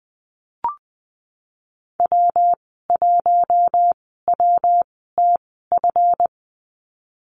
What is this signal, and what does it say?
Morse code, heard as a keyed tone, about 19 wpm, spaced as for 14 wpm, 700 Hz. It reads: W1WTF
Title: 179 Lesson 29g - Slash - Call Signs 20wpm